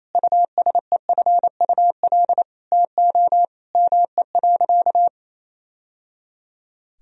Morse code, heard as a keyed tone, about 28 wpm, 700 Hz. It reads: USEFUL TO ME.
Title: Sentences from Top 1000 Words - Review Part 4 - 28wpm